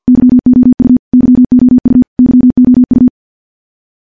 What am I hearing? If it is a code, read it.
Morse code, heard as a keyed tone, about 50 wpm, 250 Hz. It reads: YOU YOU YOU
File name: morse / morse-training-50WPM